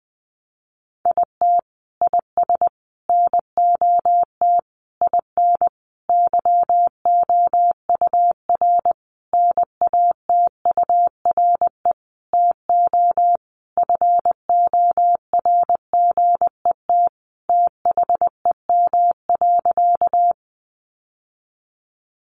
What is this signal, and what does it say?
Morse code, heard as a keyed tone, about 19 wpm, 700 Hz. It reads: IT IS NOT IN YOUR NATURE TO FORGET THEM.
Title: Sentences from Top 1200 Words 20wpm